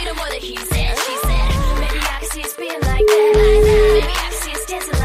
Ringback tone (Europe, 425 Hz), with noise.